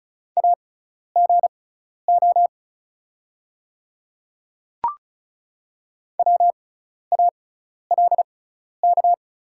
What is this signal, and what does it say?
Morse code, heard as a keyed tone, about 36 wpm, spaced as for 12 wpm, 700 Hz. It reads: AGO WALK
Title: 6x Character Spacing - Top 500 Words - 35wpm